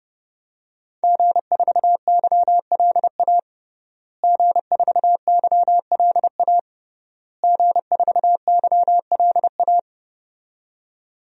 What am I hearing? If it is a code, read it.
Morse code, heard as a keyed tone, about 32 wpm, 700 Hz. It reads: G4YLA G4YLA G4YLA
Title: DXCC-100 Call Signs - 5 Characters - Repeated 3x - 30wpm